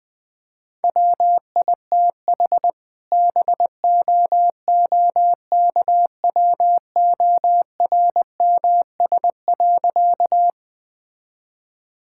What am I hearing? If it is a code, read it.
Morse code, heard as a keyed tone, about 20 wpm, 700 Hz. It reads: WITH BOOKWORMS.
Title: All-Age Appropriate Jokes - 20wpm